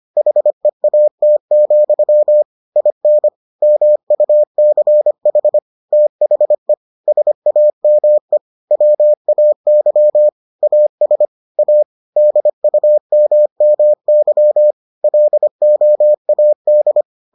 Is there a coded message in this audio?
Yes: Morse code from 0.2 to 17.0 s, heard as a keyed tone, about 25 wpm, 600 Hz. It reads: HEAT, IN MUCH THE SAME WAY AS A DUMMY LOAD